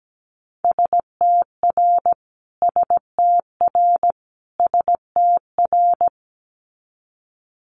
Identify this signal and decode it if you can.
Morse code, heard as a keyed tone, about 17 wpm, 700 Hz. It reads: STR STR STR
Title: Multi-letter Phonemes Repeated 3x 17wpm